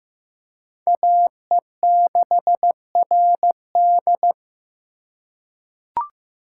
Morse code, heard as a keyed tone, about 15 wpm, 700 Hz. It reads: AE6RD
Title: USA Call Signs - 5 Characters - 8x Character Spacing - 15wpm